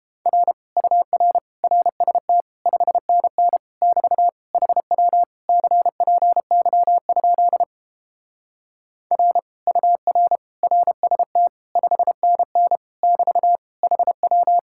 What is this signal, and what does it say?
Morse code, heard as a keyed tone, about 29 wpm, 700 Hz. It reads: R UR RST 5NN <BT> HW CPY? R UR RST 5NN <BT> HW